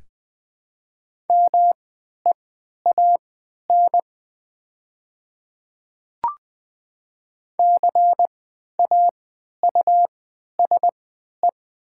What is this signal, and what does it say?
Morse code, heard as a keyed tone, about 20 wpm, spaced as for 11 wpm, 700 Hz. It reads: MEAN CAUSE
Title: 3x Character Spacing - Top 300 Words - 20wpm